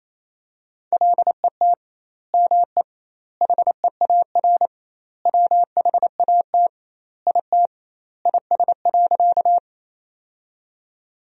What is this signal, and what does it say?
Morse code, heard as a keyed tone, about 28 wpm, 700 Hz. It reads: LET ME HEAR WHAT IT IS.
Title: 2x Word-Spacing - Sentences from Top 300 Words - 28wpm